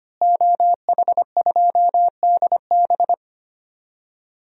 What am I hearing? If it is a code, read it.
Morse code, heard as a keyed tone, about 25 wpm, 700 Hz. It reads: OH2DB
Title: DXCC-100 Call Signs - 5 Characters - Repeated 2x - 25wpm